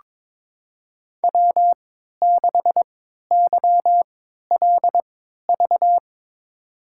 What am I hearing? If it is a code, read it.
Morse code, heard as a keyed tone, about 22 wpm, spaced as for 12 wpm, 700 Hz. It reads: W6YLV
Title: USA Call Signs - 5 Characters - 3x Character Spacing - 22wpm